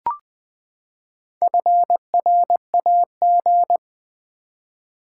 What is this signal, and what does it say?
Morse code, heard as a keyed tone, about 20 wpm, 700 Hz. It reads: FRAG
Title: Build A Word - 4 Letters At A Time - 8-Letter Words - 20wpm